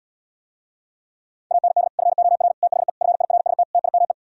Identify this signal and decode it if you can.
Morse code, heard as a keyed tone, about 77 wpm, 690 Hz. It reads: AND FOR THE FIRST TIME